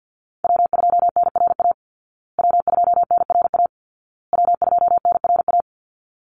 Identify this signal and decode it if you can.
Morse code, heard as a keyed tone, about 50 wpm, 700 Hz. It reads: W2NRA W2NRA W2NRA
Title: Call Signs Repeated 2x - 50wpm